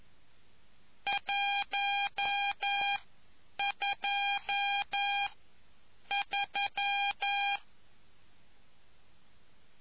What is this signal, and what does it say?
Morse code, heard as a keyed tone, about 11 wpm, spaced as for 8 wpm, 800 Hz. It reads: 12 3